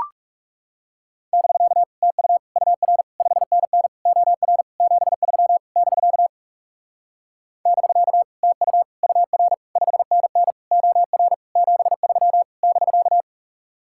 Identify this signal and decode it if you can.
Morse code, heard as a keyed tone, about 40 wpm, 700 Hz. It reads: <BK> TU UR 5NN OR 73 <BK> <BK> TU UR 5NN OR 73 <BK>